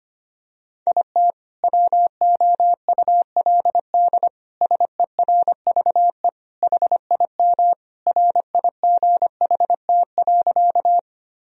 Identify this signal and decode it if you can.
Morse code, heard as a keyed tone, about 25 wpm, 700 Hz. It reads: IT WOULD SERVE HIM RIGHT.
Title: Sentences from Top 1000 Words - Review Part 2 - 25wpm